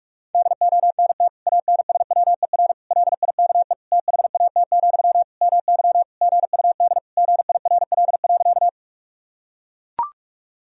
Morse code, heard as a keyed tone, about 41 wpm, 700 Hz. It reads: DONT ANSWER LIKE THAT, MY GUD GIRL.
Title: Sentences from Top 400 Words and Abbreviations 45wpm